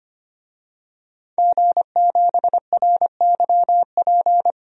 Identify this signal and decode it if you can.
Morse code, heard as a keyed tone, about 25 wpm, 700 Hz. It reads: G7RYP
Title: DXCC-100 Call Signs - 5 Characters - 25wpm